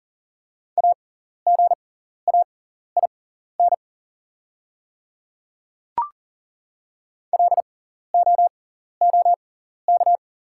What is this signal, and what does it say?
Morse code, heard as a keyed tone, about 39 wpm, spaced as for 14 wpm, 700 Hz. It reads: AGAIN LOOK
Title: 6x Character Spacing - Top 200 Words - 40wpm